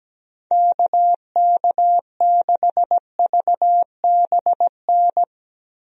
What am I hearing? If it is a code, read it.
Morse code, heard as a keyed tone, about 17 wpm, 700 Hz. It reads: KK6VBN
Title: USA Call Signs - 6 Characters - Repeated 3x - 17wpm